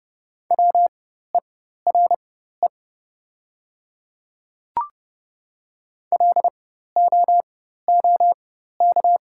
Morse code, heard as a keyed tone, about 30 wpm, spaced as for 14 wpm, 700 Hz. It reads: WERE LOOK
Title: 4x Character Spacing - Top 100 Words - 30wpm